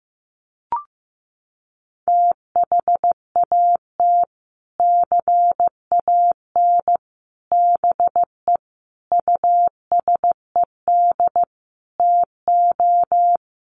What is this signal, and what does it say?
Morse code, heard as a keyed tone, about 15 wpm, 700 Hz. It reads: THAT CAN BE USED TO
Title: Sets of 5 Words Encore 15wpm